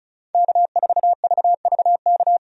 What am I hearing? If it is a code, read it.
Morse code, heard as a keyed tone, about 36 wpm, 700 Hz. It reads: K4VVK
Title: USA Call Signs - 5 Characters - 35wpm